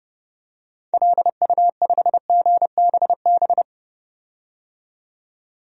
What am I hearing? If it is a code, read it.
Morse code, heard as a keyed tone, about 30 wpm, 700 Hz. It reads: LU5GBB
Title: DXCC-100 Call Signs - 6 Characters - 30wpm